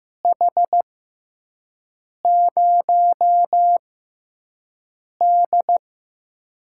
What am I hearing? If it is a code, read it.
Morse code, heard as a keyed tone, about 15 wpm, spaced as for 5 wpm, 700 Hz. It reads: H0D
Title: USA Call Signs - 6x Character Spacing - 15wpm